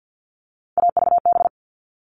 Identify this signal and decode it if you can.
Morse code, heard as a keyed tone, about 50 wpm, 700 Hz. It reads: AVB
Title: All Cut Numbers - 3-Digit 50wpm